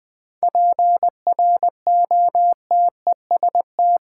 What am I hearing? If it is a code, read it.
Morse code, heard as a keyed tone, about 20 wpm, 700 Hz. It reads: PROTEST
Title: 7 Letter Words - First 3 Letters Given - 20wpm